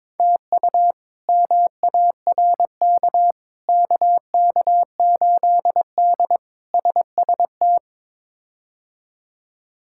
Morse code, heard as a keyed tone, about 22 wpm, 700 Hz. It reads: TU MARK KK8D SST
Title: K1USN SST QSO Practice - 22wpm